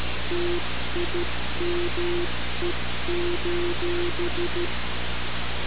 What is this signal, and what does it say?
Morse code, heard as a keyed tone, about 13 wpm, 350 Hz, with noise. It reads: TIME:
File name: radiocap / REC z